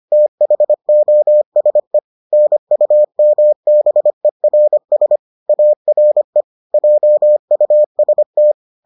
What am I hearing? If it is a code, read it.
Morse code, heard as a keyed tone, about 25 wpm, 600 Hz. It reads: THOSE NUMBERS ARE JUST